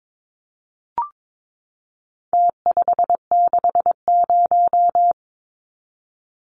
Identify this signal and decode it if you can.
Morse code, heard as a keyed tone, about 22 wpm, 700 Hz. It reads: T560
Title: Numbers - Contest Style - 4-Digit 22wpm